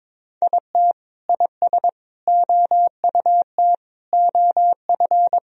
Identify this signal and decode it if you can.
Morse code, heard as a keyed tone, about 22 wpm, 700 Hz. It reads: IT IS OUT OF
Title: Sentences from Top 1000 Words - Review Part 4 - 22wpm